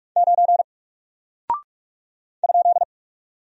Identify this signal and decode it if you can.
Morse code, heard as a keyed tone, about 44 wpm, 700 Hz. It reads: 9 ?
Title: Single Character (Letter:Number:Punctuation:Prosign) 45wpm